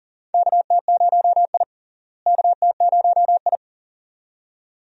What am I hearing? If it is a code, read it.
Morse code, heard as a keyed tone, about 38 wpm, 700 Hz. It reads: KT0I KT0I
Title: USA Call Signs - 4 Characters - Repeated 2x - 40wpm